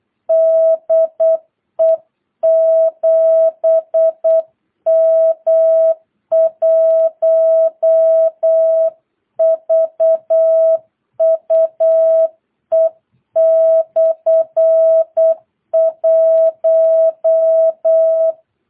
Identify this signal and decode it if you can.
Morse code, heard as a keyed tone, about 8 wpm, 650 Hz. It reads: DE 7M1VUE/1
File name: jh8cbh / 2022cw